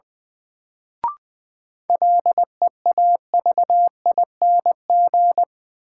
Morse code, heard as a keyed tone, about 20 wpm, 700 Hz. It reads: LEAVING
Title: Sentences from Top 1000 Words - Review Part 4 - New Word Drill - 20wpm